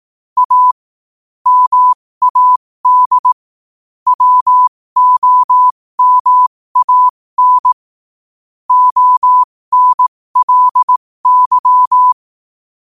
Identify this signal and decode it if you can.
Morse code, heard as a keyed tone, about 18 wpm, spaced as for 15 wpm, 1000 Hz. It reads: A MAD WOMAN ONLY
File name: qotdfiles / 15wpm